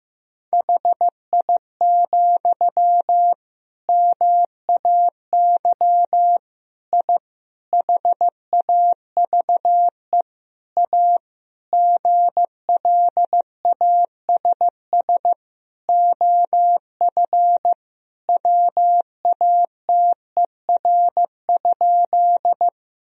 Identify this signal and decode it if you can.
Morse code, heard as a keyed tone, about 15 wpm, 700 Hz. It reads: HI, MAY I HAVE A GLASS OF WATER?
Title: Sentences from Top 1000 Words - Review Part 3 - 15wpm